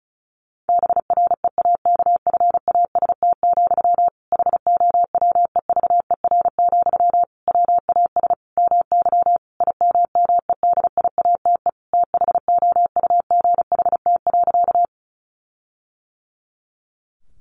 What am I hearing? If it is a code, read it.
Morse code, heard as a keyed tone, about 34 wpm, 700 Hz. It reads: BREAKFAST, HOWEVER, WAS MY IMMEDIATE THOUGHT.
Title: Sentences from Top 1500 Words 35wpm